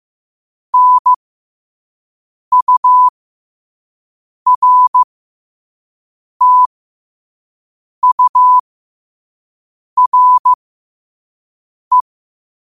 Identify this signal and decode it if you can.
Morse code, heard as a keyed tone, about 15 wpm, spaced as for 5 wpm, 1000 Hz. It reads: NURTURE
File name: qotdfiles / 5wpm